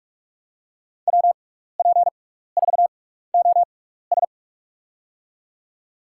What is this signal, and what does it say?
Morse code, heard as a keyed tone, about 46 wpm, spaced as for 15 wpm, 700 Hz. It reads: WP4OS
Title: USA Call Signs - 5 Characters - 6x Character Spacing - 45wpm